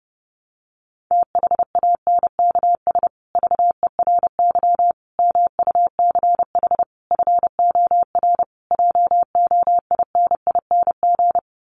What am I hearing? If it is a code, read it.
Morse code, heard as a keyed tone, about 28 wpm, 700 Hz. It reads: THANKS VERY MUCH FOR JOINING
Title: Sets of 5 Words Encore2 30wpm